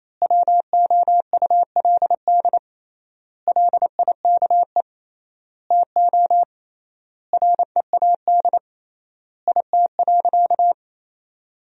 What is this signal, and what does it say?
Morse code, heard as a keyed tone, about 27 wpm, 700 Hz. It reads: WOULD LIKE TO READ IT.
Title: 3x Word-Spacing - Sentences from Top 500 Words - 28wpm